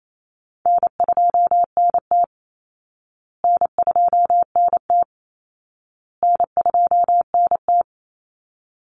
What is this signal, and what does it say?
Morse code, heard as a keyed tone, about 28 wpm, 700 Hz. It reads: N2NT N2NT N2NT
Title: Easy Call Signs Repeated 3x - 28wpm